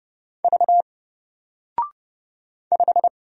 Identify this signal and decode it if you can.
Morse code, heard as a keyed tone, about 30 wpm, 700 Hz. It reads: V 5
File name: Single-Letter-Number-30wpm